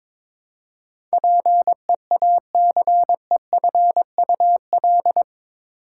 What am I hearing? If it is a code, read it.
Morse code, heard as a keyed tone, about 22 wpm, 700 Hz. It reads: PEACEFUL
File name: practice / 22wpm / Eight Letter Words - 22wpm